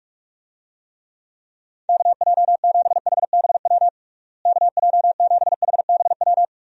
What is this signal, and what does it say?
Morse code, heard as a keyed tone, about 46 wpm, 700 Hz. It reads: KJ7HBW KJ7HBW
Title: USA Call Signs - 6 Characters - Repeated 3x - 45wpm